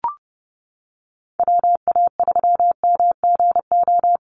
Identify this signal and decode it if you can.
Morse code, heard as a keyed tone, about 30 wpm, 700 Hz. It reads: WA3MGO
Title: USA Call Signs Repeated 3x 30wpm